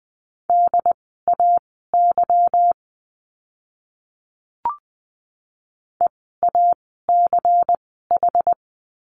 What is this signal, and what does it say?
Morse code, heard as a keyed tone, about 20 wpm, spaced as for 14 wpm, 700 Hz. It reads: DAY EACH
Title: 192 Lesson 31f - Letter v - Words 20wpm